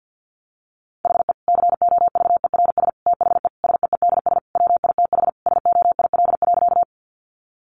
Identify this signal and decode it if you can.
Morse code, heard as a keyed tone, about 50 wpm, 700 Hz. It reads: HE COVERS THE SEEDS WITH SOIL.
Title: Easy Phrases 50wpm